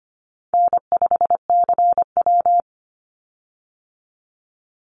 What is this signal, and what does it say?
Morse code, heard as a keyed tone, about 25 wpm, 700 Hz. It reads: N5CW
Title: Easy Call Signs Repeated 3x - 25wpm